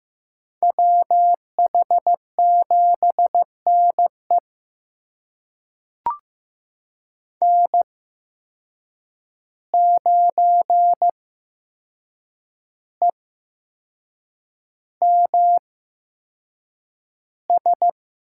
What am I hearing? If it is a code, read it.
Morse code, heard as a keyed tone, about 15 wpm, spaced as for 6 wpm, 700 Hz. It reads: WH7NE N9EMS
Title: USA Call Signs - 5 Characters - 8x Character Spacing - 15wpm